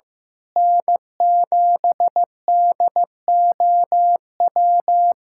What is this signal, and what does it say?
Morse code, heard as a keyed tone, about 15 wpm, 700 Hz. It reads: N7DOW
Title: USA Callsign DE Callsign - 5 Characters - 15wpm